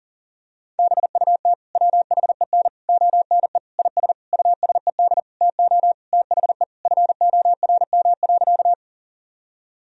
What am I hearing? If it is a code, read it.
Morse code, heard as a keyed tone, about 40 wpm, 700 Hz. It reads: BUT WHEN ONE IS USED TO THE FORM.
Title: Sentences from Top 1000 Words - Review Part 1 - 40wpm